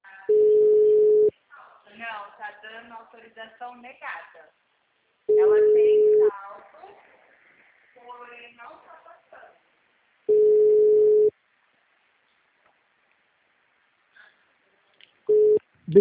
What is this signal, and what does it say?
Ringback tone (Europe, 425 Hz), with speech.